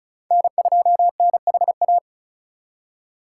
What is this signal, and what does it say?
Morse code, heard as a keyed tone, about 34 wpm, 700 Hz. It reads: N2NHA
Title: USA Call Signs - 2x Character Spacing - 35wpm